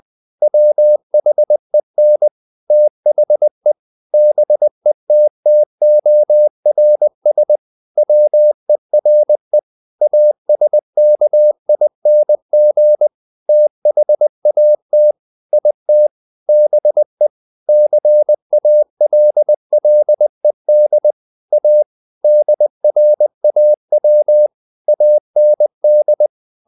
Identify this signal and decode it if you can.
Morse code, heard as a keyed tone, about 20 wpm, 600 Hz. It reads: WHEN THE BETTORS WERE ASKING THAT IT BE CALLED A DRAW AND